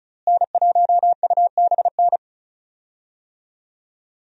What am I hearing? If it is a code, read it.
Morse code, heard as a keyed tone, about 35 wpm, 700 Hz. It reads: N1UBN
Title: USA Call Signs - 5 Characters - Repeated 2x - 35wpm